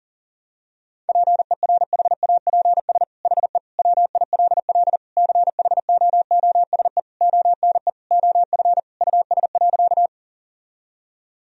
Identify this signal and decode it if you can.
Morse code, heard as a keyed tone, about 40 wpm, 700 Hz. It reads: PERHAPS HE WILL CHOOSE ONE OF US.
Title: Sentences from Top 1000 Words - Review Part 4 - 40wpm